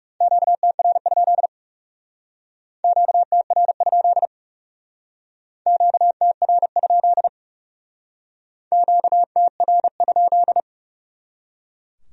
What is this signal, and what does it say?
Morse code, heard as a keyed tone, about 38 wpm, 700 Hz. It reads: QTR? QTR? QTR? QTR?